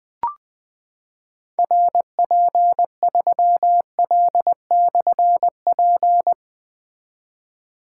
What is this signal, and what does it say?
Morse code, heard as a keyed tone, about 20 wpm, 700 Hz. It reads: RP3L/P
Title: Difficult Call Signs 20wpm